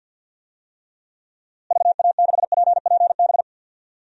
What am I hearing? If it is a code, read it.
Morse code, heard as a keyed tone, about 47 wpm, 700 Hz. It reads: VA6PPB
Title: DXCC-100 Call Signs - 6 Characters - Repeated 2x - 50wpm